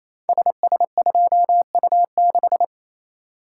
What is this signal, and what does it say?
Morse code, heard as a keyed tone, about 29 wpm, 700 Hz. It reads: SS2U6
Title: Code Groups - 5 Letter:Numbers - 28wpm